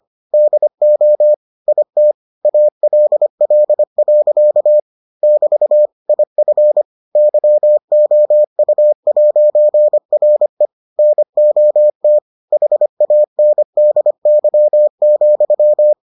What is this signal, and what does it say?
Morse code, heard as a keyed tone, about 25 wpm, 600 Hz. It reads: DO IT ALL. <BT> IF YOU'RE NOT HANDY,